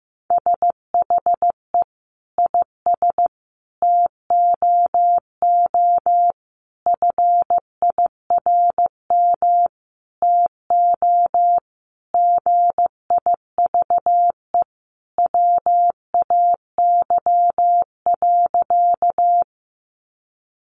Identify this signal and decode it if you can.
Morse code, heard as a keyed tone, about 15 wpm, 700 Hz. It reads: SHE IS TOO FIRM TO GIVE WAY.
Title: Sentences from Top 1400 Words 15wpm